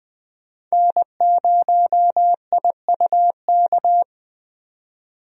Morse code, heard as a keyed tone, about 21 wpm, 700 Hz. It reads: N0IUK
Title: USA Call Signs - 5 Characters - 20wpm